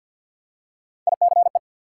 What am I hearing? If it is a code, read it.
Morse code, heard as a keyed tone, about 53 wpm, 700 Hz. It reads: ICE
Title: Three Letter Words - 50wpm